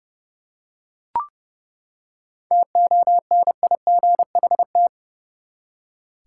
Morse code, heard as a keyed tone, about 30 wpm, 700 Hz. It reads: TONIGHT
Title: Seven Letter Words - 30wpm